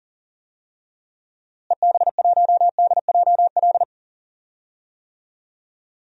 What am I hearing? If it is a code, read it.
Morse code, heard as a keyed tone, about 40 wpm, 700 Hz. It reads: EB1DJL
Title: DXCC-100 Call Signs - 6 Characters - Repeated 2x - 40wpm